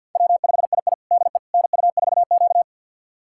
Morse code, heard as a keyed tone, about 50 wpm, 700 Hz. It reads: W5II DE NU4Q
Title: USA Callsign DE Callsign - 4 Characters - 50wpm